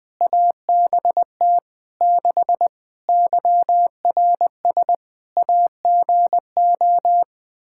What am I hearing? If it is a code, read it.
Morse code, heard as a keyed tone, about 21 wpm, 700 Hz. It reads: ABT 6 YRS AGO
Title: Practice QSO - Exchange 3 - More Info - 20wpm